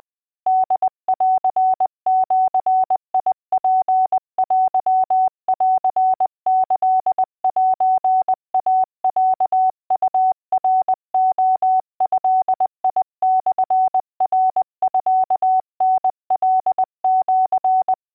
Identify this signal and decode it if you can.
Morse code (Japanese Wabun code), heard as a keyed tone, about 20 wpm, 750 Hz. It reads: ホンジツテンキセイロウナレドモナミタカシ